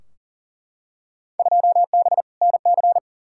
Morse code, heard as a keyed tone, about 40 wpm, 700 Hz. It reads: 2B NC
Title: Field Day Exchanges Encore - 40wpm